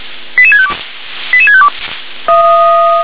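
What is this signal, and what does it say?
Five-tone selective call (ZVEI), heard as quick runs of five tones, tones of about 70 ms. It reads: 88652 88652
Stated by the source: ZVEI five-tone selective call received by SDR.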